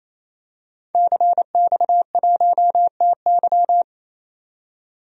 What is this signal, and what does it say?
Morse code, heard as a keyed tone, about 28 wpm, 700 Hz. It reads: CX1TY
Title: DXCC-100 Call Signs - 5 Characters - 28wpm